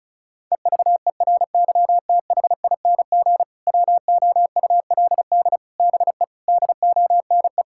Morse code, heard as a keyed tone, about 36 wpm, 700 Hz. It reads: EVERYTHING WOULD BE DONE
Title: Sentences from Top 1000 Words - Review Part 5 - 35wpm